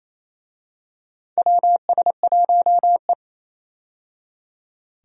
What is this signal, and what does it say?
Morse code, heard as a keyed tone, about 28 wpm, 700 Hz. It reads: WS1E
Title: USA Call Signs - 4 Characters - 28wpm